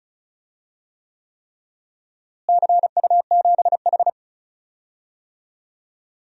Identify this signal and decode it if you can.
Morse code, heard as a keyed tone, about 36 wpm, 700 Hz. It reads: CU7H
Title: DXCC-100 Call Signs - 4 Characters - Repeated 3x - 35wpm